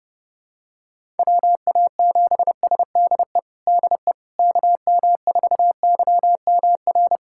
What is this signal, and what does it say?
Morse code, heard as a keyed tone, about 32 wpm, 700 Hz. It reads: WA7SDE DE KM4YMR
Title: USA Callsign DE Callsign - 6 Characters - 30wpm